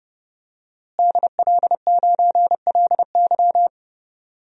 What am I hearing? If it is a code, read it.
Morse code, heard as a keyed tone, about 30 wpm, 700 Hz. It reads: DL9LY
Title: DXCC-100 Call Signs - 5 Characters - Repeated 2x - 30wpm